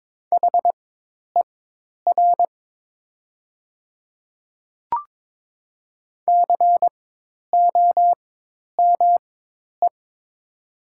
Morse code, heard as a keyed tone, about 22 wpm, spaced as for 10 wpm, 700 Hz. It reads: HER COME
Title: 4x Character Spacing - Top 100 Words - 22wpm